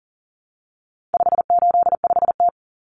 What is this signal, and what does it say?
Morse code, heard as a keyed tone, about 40 wpm, 700 Hz. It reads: 585T
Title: Common Cut Numbers - 4-Digit 40wpm